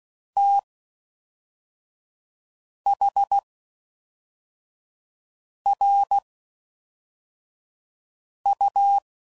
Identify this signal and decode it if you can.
Morse code, heard as a keyed tone, about 16 wpm, spaced as for 4 wpm, 800 Hz. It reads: THRU